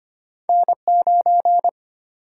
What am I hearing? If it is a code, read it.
Morse code, heard as a keyed tone, about 25 wpm, 700 Hz. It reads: N9
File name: Two Letter-Number Combinations - 25wpm